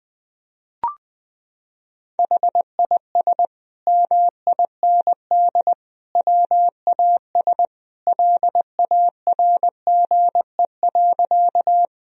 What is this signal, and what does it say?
Morse code, heard as a keyed tone, about 21 wpm, 700 Hz. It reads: HIS MIND WAS LARGE.
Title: Sentences from Top 1000 Words - Review Part 1 - 20wpm